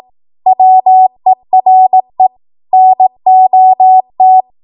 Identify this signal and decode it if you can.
Morse code, heard as a keyed tone, about 18 wpm, 750 Hz. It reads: WERE NOT
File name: Archive / 18 WPM / 18WPM